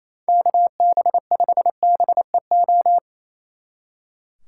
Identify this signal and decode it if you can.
Morse code, heard as a keyed tone, about 35 wpm, 700 Hz. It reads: KB5BEO
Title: Speed Racing - USA 2x3 Call Signs 28wpm